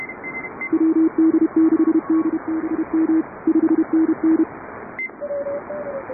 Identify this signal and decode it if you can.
Morse code, heard as a keyed tone, about 31 wpm, 310 Hz, with radio noise. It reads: WD6DBM 5NN